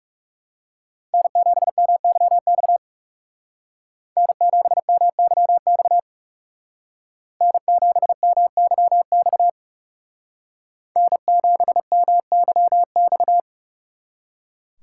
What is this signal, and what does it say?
Morse code, heard as a keyed tone, about 35 wpm, 700 Hz. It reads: N7MYX N7MYX N7MYX N7MYX
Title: Speed Racing - USA 1x3 Call Signs 30wpm